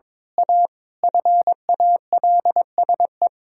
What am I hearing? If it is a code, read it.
Morse code, heard as a keyed tone, about 22 wpm, 700 Hz. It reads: A FALSE
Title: Sentences from Top 1600 Words 22wpm